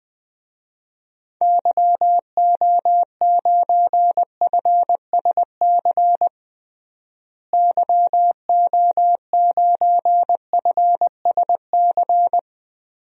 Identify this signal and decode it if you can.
Morse code, heard as a keyed tone, about 20 wpm, 700 Hz. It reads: YO9FSC YO9FSC
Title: DXCC-100 Call Signs - 6 Characters - Repeated 2x - 20wpm